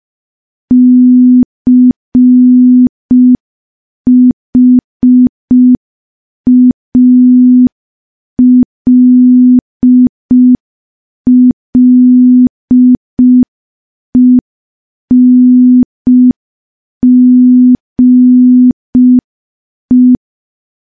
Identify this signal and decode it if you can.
Morse code, heard as a keyed tone, about 5 wpm, 250 Hz. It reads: CHALLENGE
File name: morse / morse-training-05WPM